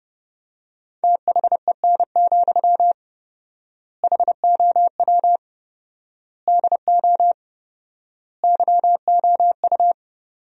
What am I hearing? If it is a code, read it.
Morse code, heard as a keyed tone, about 30 wpm, 700 Hz. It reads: THEN, HOW DO YOU
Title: 4x Word-Spacing - Sentences from Top 100 Words - 30wpm